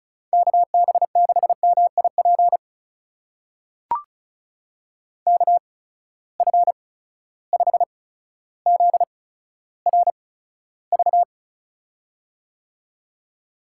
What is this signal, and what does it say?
Morse code, heard as a keyed tone, about 36 wpm, spaced as for 15 wpm, 700 Hz. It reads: KB6MIP KF5ZRV
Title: USA Call Signs - 8x Character Spacing - 35wpm